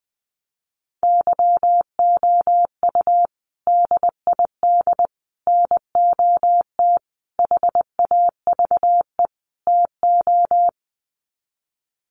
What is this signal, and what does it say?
Morse code, heard as a keyed tone, about 20 wpm, 700 Hz. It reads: YOU DID NOT HAVE TO
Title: Sets of 5 Words Encore 20wpm